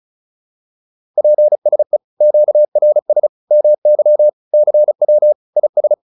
Morse code, heard as a keyed tone, about 34 wpm, 600 Hz. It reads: PSE QRS MY CW IS